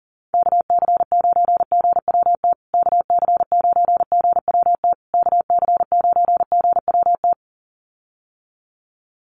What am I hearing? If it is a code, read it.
Morse code, heard as a keyed tone, about 40 wpm, 700 Hz. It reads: KC9GWT KC9GWT KC9GWT
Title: USA Call Signs Repeated 3x 40wpm